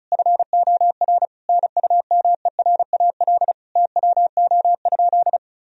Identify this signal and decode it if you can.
Morse code, heard as a keyed tone, about 36 wpm, 700 Hz. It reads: FOR NUMERAL TWO?
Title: Sentences from Top 1000 Words - Review Part 2 - 35wpm